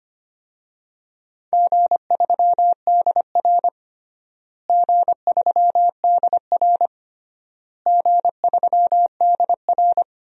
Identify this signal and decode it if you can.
Morse code, heard as a keyed tone, about 25 wpm, 700 Hz. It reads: G3DR G3DR G3DR
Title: DXCC-100 Call Signs - 4 Characters - Repeated 3x - 25wpm